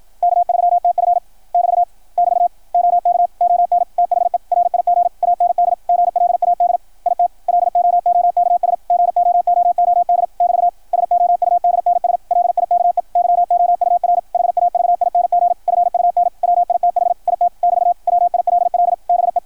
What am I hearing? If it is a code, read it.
Morse code, heard as a keyed tone, about 53 wpm, 660 Hz, with radio noise. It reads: K3TF <BT> <BT> OK ON THE RIG AND GLAD IT LOOKS GOOOD <BT> SOUDNS LIKE YOUR HAVING FUN WITH IT <BT> WILL BE
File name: archived / CRSnet